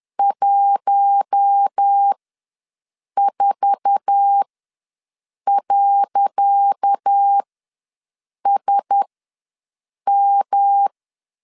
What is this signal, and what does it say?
Morse code, heard as a keyed tone, about 10 wpm, spaced as for 6 wpm, 800 Hz. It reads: 14.SM
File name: LEZIONI / 10 SIGNS / 10C DOT ws 10wpm 4213